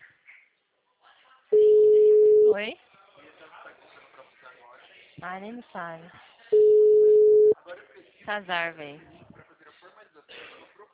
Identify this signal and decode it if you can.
Ringback tone (Europe, 425 Hz), with speech.